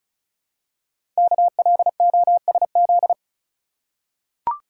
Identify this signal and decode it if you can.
Morse code, heard as a keyed tone, about 36 wpm, 700 Hz. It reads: KLOSZ
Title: Code Groups - 5 Letters - 35wpm